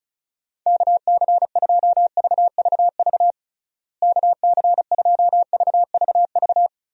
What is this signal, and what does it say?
Morse code, heard as a keyed tone, about 34 wpm, 700 Hz. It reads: KC2VVV KC2VVV
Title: USA Call Signs - 6 Characters - Repeated 2x - 35wpm